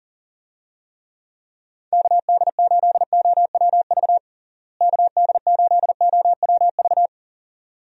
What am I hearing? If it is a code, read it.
Morse code, heard as a keyed tone, about 40 wpm, 700 Hz. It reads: KD8OWV KD8OWV
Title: USA Call Signs - 6 Characters - Repeated 2x - 40wpm